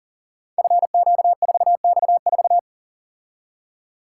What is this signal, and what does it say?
Morse code, heard as a keyed tone, about 42 wpm, 690 Hz. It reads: FQ4X4
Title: Code Groups - 5 Letter:Numbers - 40wpm